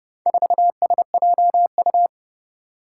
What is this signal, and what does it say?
Morse code, heard as a keyed tone, about 30 wpm, 700 Hz. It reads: 4SJU
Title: Four Challenging Letter-Number Combinations 30wpm